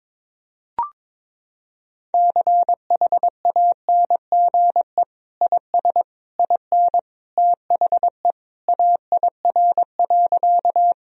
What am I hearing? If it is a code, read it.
Morse code, heard as a keyed tone, about 22 wpm, 700 Hz. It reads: CHANGE IS IN THE AIR.
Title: Sentences from Top 1000 Words - Review Part 1 - 22wpm